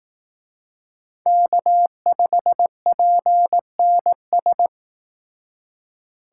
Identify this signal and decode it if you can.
Morse code, heard as a keyed tone, about 18 wpm, 700 Hz. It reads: K5PNS
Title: Speed Racing - USA 1x3 Call Signs 15wpm